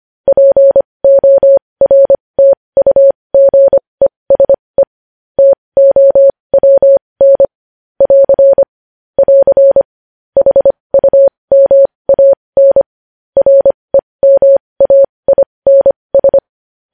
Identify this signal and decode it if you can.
Morse code, heard as a keyed tone, about 25 wpm, spaced as for 20 wpm, 550 Hz. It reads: PORTUGESE TOWN <AR> <AR> HUMAN REMAINS
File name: podcast / News headlines-1x-25x20